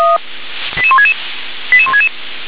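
Five-tone selective call (ZVEI), heard as quick runs of five tones, tones of about 70 ms. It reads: 88166 88166
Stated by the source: ZVEI five-tone call received on SDR.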